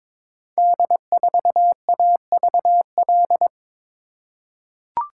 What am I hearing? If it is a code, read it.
Morse code, heard as a keyed tone, about 22 wpm, 700 Hz. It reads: D4AVL
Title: Code Groups - 5 Letter:Numbers - 22wpm